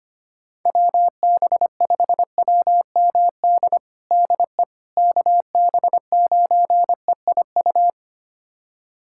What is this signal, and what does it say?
Morse code, heard as a keyed tone, about 25 wpm, 700 Hz. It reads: WB5WMD DE KB9EIU
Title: USA Callsign DE Callsign - 6 Characters - 25wpm